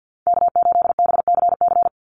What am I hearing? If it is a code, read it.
Morse code, heard as a keyed tone, about 49 wpm, 700 Hz. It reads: K8BCC
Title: USA Call Signs 50wpm